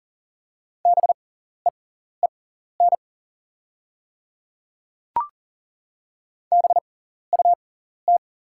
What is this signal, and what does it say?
Morse code, heard as a keyed tone, about 41 wpm, spaced as for 14 wpm, 690 Hz. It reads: BEEN BUT
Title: 6x Character Spacing - Top 100 Words - 40wpm